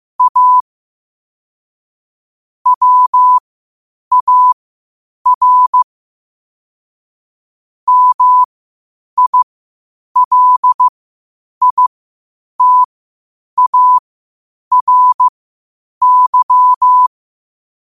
Morse code, heard as a keyed tone, about 15 wpm, spaced as for 8 wpm, 1000 Hz. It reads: A WAR MILITARY
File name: qotdfiles / 8wpm